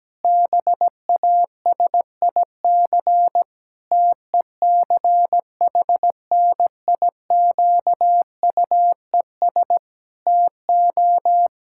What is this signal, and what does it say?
Morse code, heard as a keyed tone, about 17 wpm, 700 Hz. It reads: BASIC TECHNIQUES TO
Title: Pangram Challenge v2 - 17wpm